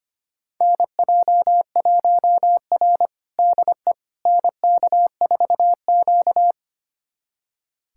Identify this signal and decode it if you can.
Morse code, heard as a keyed tone, about 25 wpm, 700 Hz. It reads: NJ1R DE NK4Q
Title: USA Callsign DE Callsign - 4 Characters - 25wpm